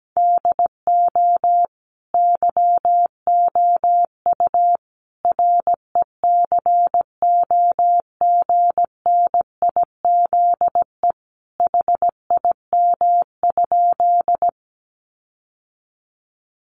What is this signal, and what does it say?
Morse code, heard as a keyed tone, about 17 wpm, 700 Hz. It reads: DO YOU RECOGNIZE HIM?
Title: Sentences from Top 1400 Words 17wpm